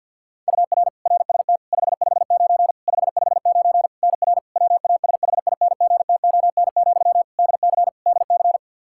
Morse code, heard as a keyed tone, about 50 wpm, 700 Hz. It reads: UR RST 559 559 NR WASHINGTON, DC DC
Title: Practice QSO - Exchange 1 - Beginning - 50wpm